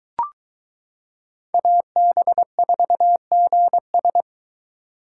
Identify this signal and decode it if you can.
Morse code, heard as a keyed tone, about 22 wpm, 700 Hz. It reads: AB4GS